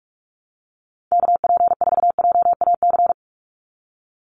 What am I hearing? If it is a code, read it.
Morse code, heard as a keyed tone, about 44 wpm, 690 Hz. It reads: KP4JAC
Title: USA Call Signs 45wpm